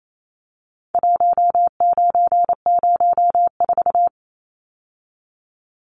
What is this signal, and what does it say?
Morse code, heard as a keyed tone, about 28 wpm, 700 Hz. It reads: 1904
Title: Numbers - 4-Digit 28wpm